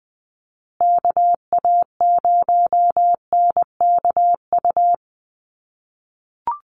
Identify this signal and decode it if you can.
Morse code, heard as a keyed tone, about 20 wpm, 700 Hz. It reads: KA0NKU
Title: USA Call Signs Repeated 3x 20wpm